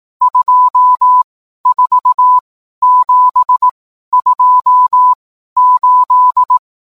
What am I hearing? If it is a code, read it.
Morse code, heard as a keyed tone, about 18 wpm, 1000 Hz. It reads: 2 4 7 2 8
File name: games / morse